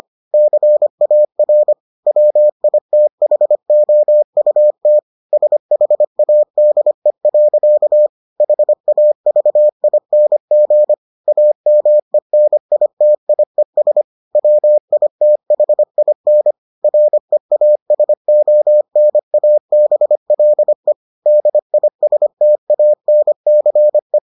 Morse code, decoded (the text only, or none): CAR WITHOUT SHADE. HAVING AMENITIES WITHIN REASONABLE DISTANCE